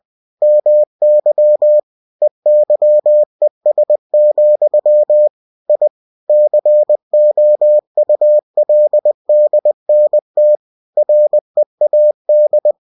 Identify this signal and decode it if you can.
Morse code, heard as a keyed tone, about 20 wpm, 600 Hz. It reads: MY EYES, I COULDNT READ
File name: cw-books / winnie-the-pooh / winnie-20wpm-page-0098